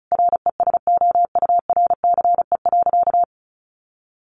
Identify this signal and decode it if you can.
Morse code, heard as a keyed tone, about 34 wpm, 700 Hz. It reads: RESOURCE.
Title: Sentences from Top 1300 Words 35wpm